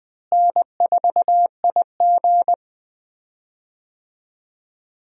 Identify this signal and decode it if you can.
Morse code, heard as a keyed tone, about 20 wpm, 700 Hz. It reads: N4IG
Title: USA Call Signs - 4 Characters - Repeated 2x - 20wpm